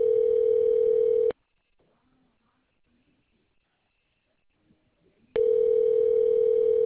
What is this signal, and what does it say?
Ringback tone (North America), 440 + 480 Hz.